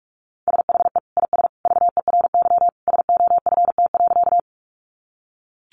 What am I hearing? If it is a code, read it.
Morse code, heard as a keyed tone, about 44 wpm, 700 Hz. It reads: SHE IS VERY SOFT.
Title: Easy Phrases 45wpm